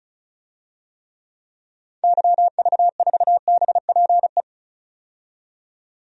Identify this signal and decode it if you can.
Morse code, heard as a keyed tone, about 36 wpm, 700 Hz. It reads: YV4BPE
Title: DXCC-100 Call Signs - 6 Characters - 35wpm